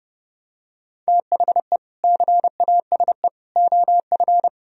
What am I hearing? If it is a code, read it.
Morse code, heard as a keyed tone, about 32 wpm, 700 Hz. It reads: THE CASE OF